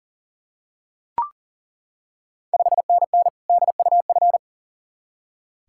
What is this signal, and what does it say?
Morse code, heard as a keyed tone, about 38 wpm, 690 Hz. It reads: 5NN DUF
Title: Ontario QSO Party Exchange 40wpm